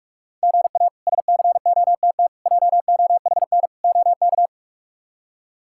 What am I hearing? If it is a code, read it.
Morse code, heard as a keyed tone, about 42 wpm, 700 Hz. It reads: GA SCOTT JOHN OK
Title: K1USN SST QSO Practice - 45wpm